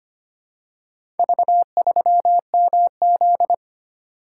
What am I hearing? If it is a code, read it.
Morse code, heard as a keyed tone, about 25 wpm, 700 Hz. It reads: V3MZ
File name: practice / 25wpm / Four Letter-Number Combinations - 25wpm